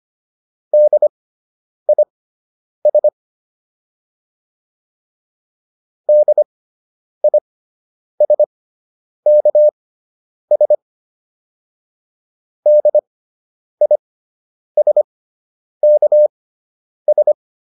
Morse code, heard as a keyed tone, about 25 wpm, spaced as for 9 wpm, 600 Hz. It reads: DIS DISKS DISKS